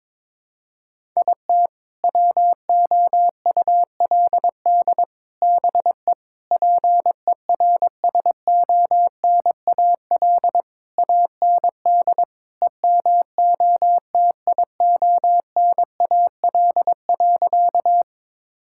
Morse code, heard as a keyed tone, about 22 wpm, 700 Hz. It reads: IT WOULD BE PERSONAL AND EMOTIONAL.
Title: Sentences from Top 1000 Words - Review Part 5 - 22wpm